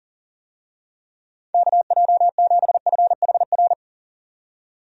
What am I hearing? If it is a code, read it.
Morse code, heard as a keyed tone, about 40 wpm, 700 Hz. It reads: KJ7FHR
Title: USA Call Signs - 6 Characters - 40wpm